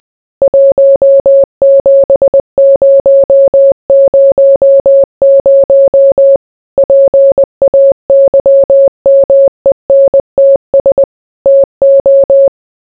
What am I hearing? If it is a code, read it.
Morse code, heard as a keyed tone, about 20 wpm, 550 Hz. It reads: 17000 PAYMENTS TO